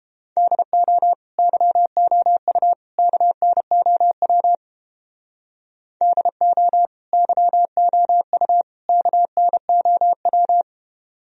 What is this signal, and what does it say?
Morse code, heard as a keyed tone, about 29 wpm, 700 Hz. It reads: DO YOU KNOW DO YOU KNOW